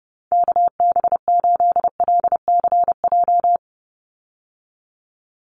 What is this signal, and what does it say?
Morse code, heard as a keyed tone, about 30 wpm, 700 Hz. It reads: KB8LCJ
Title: USA Call Signs 30wpm